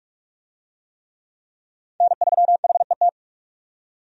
Morse code, heard as a keyed tone, about 46 wpm, 700 Hz. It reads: N3HET
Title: USA Call Signs - 5 Characters - 45wpm